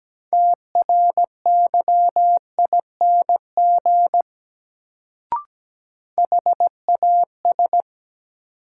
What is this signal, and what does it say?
Morse code, heard as a keyed tone, about 17 wpm, 700 Hz. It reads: TRYING HAS
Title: Sets of 2 Words - New Word Drill - 17wpm